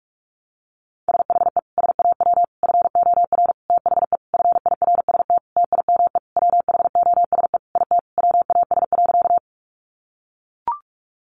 Morse code, heard as a keyed tone, about 44 wpm, 700 Hz. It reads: SHE SAW FOR THE FIRST TIME WHOSE IT WAS.
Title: Sentences from Top 1200 Words 45wpm